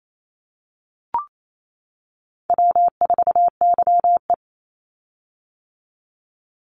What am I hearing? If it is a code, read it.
Morse code, heard as a keyed tone, about 28 wpm, 700 Hz. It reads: W4YE
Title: Easy Call Signs - 28wpm